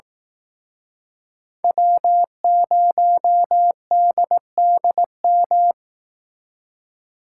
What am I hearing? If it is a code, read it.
Morse code, heard as a keyed tone, about 19 wpm, 700 Hz. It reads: W0DDM